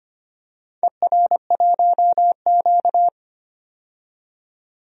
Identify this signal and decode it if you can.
Morse code, heard as a keyed tone, about 25 wpm, 690 Hz. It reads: ER1Q
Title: DXCC-100 Call Signs - 4 Characters - 25wpm